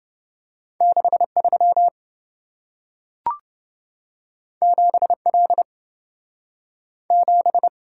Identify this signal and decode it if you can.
Morse code, heard as a keyed tone, about 28 wpm, 700 Hz. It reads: 63 7L 7